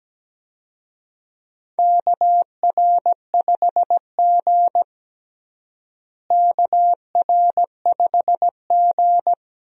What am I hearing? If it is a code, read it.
Morse code, heard as a keyed tone, about 17 wpm, 700 Hz. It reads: KR5G KR5G
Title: USA Call Signs - 4 Characters - Repeated 2x - 17wpm